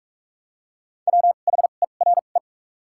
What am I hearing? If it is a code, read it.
Morse code, heard as a keyed tone, about 45 wpm, spaced as for 32 wpm, 700 Hz. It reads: WHERE
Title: 2x Character Spacing - Top 200 Words - 45wpm